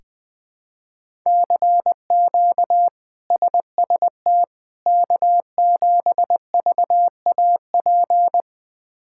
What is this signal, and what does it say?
Morse code, heard as a keyed tone, about 20 wpm, 700 Hz. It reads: CQ SST K7VAP
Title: K1USN SST QSO Practice - 20wpm